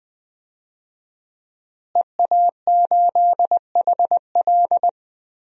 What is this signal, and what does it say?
Morse code, heard as a keyed tone, about 20 wpm, 700 Hz. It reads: EA8HL
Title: DXCC-100 Call Signs - 5 Characters - Repeated 2x - 20wpm